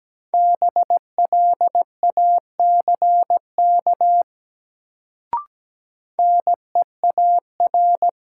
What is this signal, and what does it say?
Morse code, heard as a keyed tone, about 17 wpm, 700 Hz. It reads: BLACK NEAR
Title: Sets of 5 Words Encore - New Word Drill - 17wpm